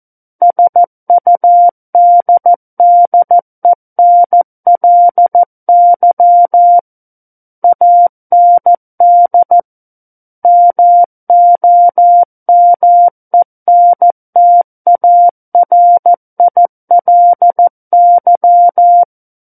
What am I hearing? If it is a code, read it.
Morse code, heard as a keyed tone, about 14 wpm, 700 Hz. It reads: SUDDENLY AND MOMENTARILY
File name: Book One / 05-Chapter 5 - 14 WPM